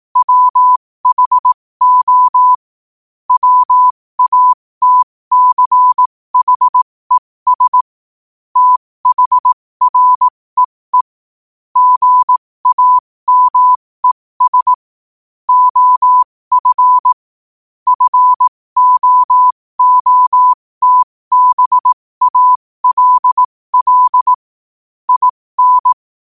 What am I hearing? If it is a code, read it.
Morse code, heard as a keyed tone, about 18 wpm, spaced as for 15 wpm, 1000 Hz. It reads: WHO WATCHES THREE GAMES OF FOOTBALL IN